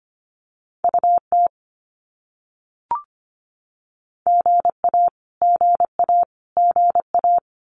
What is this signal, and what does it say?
Morse code, heard as a keyed tone, about 25 wpm, 700 Hz. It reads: UT GA GA GA
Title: US States Repeated 3x 25wpm